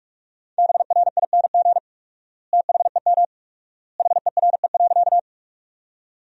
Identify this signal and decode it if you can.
Morse code, heard as a keyed tone, about 46 wpm, 690 Hz. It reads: BRING THEM HERE.
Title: 4x Word-Spacing - Sentences from Top 500 Words - 45wpm